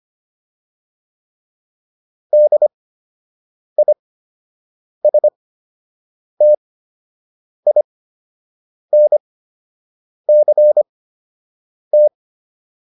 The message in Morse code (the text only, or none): DISTINCT